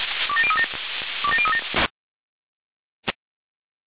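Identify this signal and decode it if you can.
Five-tone selective call (ZVEI), heard as quick runs of five tones, tones of about 70 ms. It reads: 27037 27037
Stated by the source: ZVEI five-tone selective call 27084, received on SDR.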